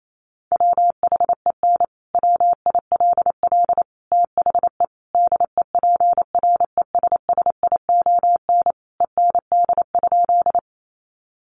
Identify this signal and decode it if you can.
Morse code, heard as a keyed tone, about 28 wpm, 700 Hz. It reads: WHEN WILL THE DEPRESSION END?
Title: Sentences from Top 1300 Words 28wpm